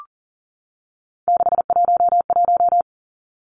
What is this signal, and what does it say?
Morse code, heard as a keyed tone, about 40 wpm, 700 Hz. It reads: 611
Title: Numbers - 3-Digit 40wpm